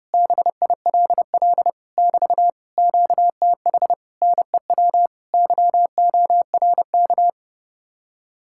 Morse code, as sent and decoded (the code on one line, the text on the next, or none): -... .. .-.. .-.. / -...- / --.- - .... / -. .-- / -.-- --- .-. -.-
BILL <BT> QTH NEW YORK